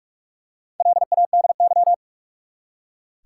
Morse code, heard as a keyed tone, about 44 wpm, 700 Hz. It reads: LADY
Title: Four Letter Words - 45wpm